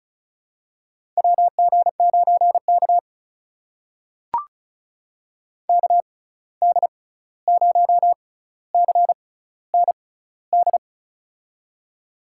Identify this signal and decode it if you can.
Morse code, heard as a keyed tone, about 36 wpm, spaced as for 16 wpm, 700 Hz. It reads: WG9K KD0CND
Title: USA Call Signs - 6x Character Spacing - 35wpm